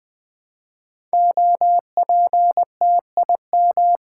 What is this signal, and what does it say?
Morse code, heard as a keyed tone, about 20 wpm, 700 Hz. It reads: OPTIM
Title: Build A Word - 5 Letters At A Time - 10-Letter Words - 20wpm